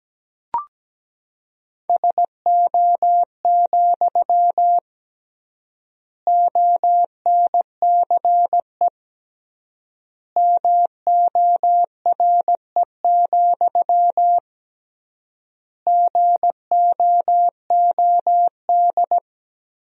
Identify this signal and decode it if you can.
Morse code, heard as a keyed tone, about 17 wpm, 700 Hz. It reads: SO, ONCE MORE, GOOD
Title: 3x Word-Spacing - Sentences from Top 300 Words - 17wpm